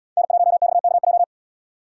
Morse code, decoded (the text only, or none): N1XKJ